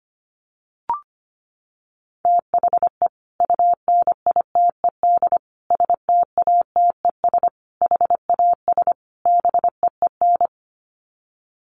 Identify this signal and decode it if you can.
Morse code, heard as a keyed tone, about 25 wpm, 700 Hz. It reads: THE UNITED STATES HAS BEEN
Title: Sets of 5 Words Encore2 25wpm